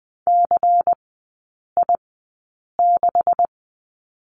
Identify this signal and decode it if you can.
Morse code, heard as a keyed tone, about 20 wpm, spaced as for 8 wpm, 700 Hz. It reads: CI6
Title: 133 Lesson 22e - Question Mark - Three Characters 20wpm